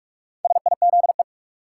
Morse code, heard as a keyed tone, about 42 wpm, 700 Hz. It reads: SIZE
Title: Sets of 3 Words - New Word Drill - 45wpm